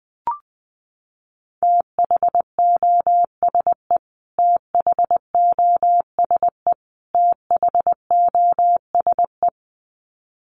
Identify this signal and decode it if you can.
Morse code, heard as a keyed tone, about 20 wpm, 700 Hz. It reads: THOSE THOSE THOSE
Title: Top 500 Words Repeated 20wpm